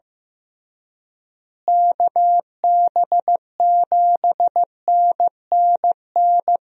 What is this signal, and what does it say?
Morse code, heard as a keyed tone, about 15 wpm, 700 Hz. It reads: KB7NNN
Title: USA Call Signs - 6 Characters - Repeated 2x - 15wpm